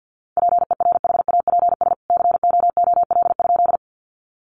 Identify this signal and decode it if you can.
Morse code, heard as a keyed tone, about 50 wpm, 700 Hz. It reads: PERHAPS COOL?
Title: Sentences from Top 1200 Words 50wpm